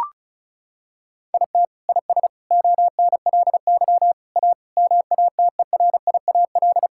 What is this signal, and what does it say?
Morse code, heard as a keyed tone, about 36 wpm, 700 Hz. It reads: IT IS ONLY A MATERIAL
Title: Sentences from Top 1000 Words - Review Part 5 - 35wpm